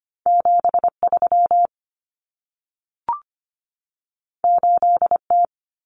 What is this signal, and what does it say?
Morse code, heard as a keyed tone, about 25 wpm, 700 Hz. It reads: 73 8T
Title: Common Cut Numbers - 2-Digit 25wpm